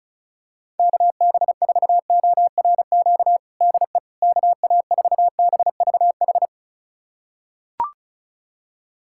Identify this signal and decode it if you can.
Morse code, heard as a keyed tone, about 36 wpm, 700 Hz. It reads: KB4ORQ DE KA4BVH
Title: USA Callsign DE Callsign - 6 Characters - 35wpm